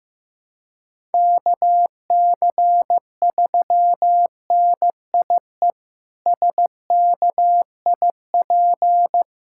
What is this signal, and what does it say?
Morse code, heard as a keyed tone, about 15 wpm, 700 Hz. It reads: KC3NIE SKIP
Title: K1USN SST QSO Practice - 15wpm